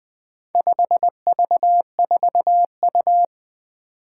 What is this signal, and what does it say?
Morse code, heard as a keyed tone, about 20 wpm, 700 Hz. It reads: 5V4U
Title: Four Challenging Letter-Number Combinations 20wpm